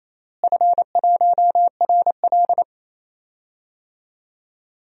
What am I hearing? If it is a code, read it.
Morse code, heard as a keyed tone, about 27 wpm, 700 Hz. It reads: F1RL
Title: DXCC-100 Call Signs - 4 Characters - Repeated 2x - 28wpm